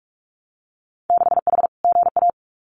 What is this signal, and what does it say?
Morse code, heard as a keyed tone, about 46 wpm, 700 Hz. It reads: 6H GA
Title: Winter Field Day Exchanges Encore 45wpm